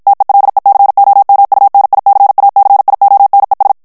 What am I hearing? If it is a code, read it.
Morse code, heard as a keyed tone, about 56 wpm, 800 Hz. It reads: TELECOMUNICACIONES
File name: morse mp3 / ESPAGNOL TEXTE EN CLAIR 2 VITESSE 50 MOTS MINUTES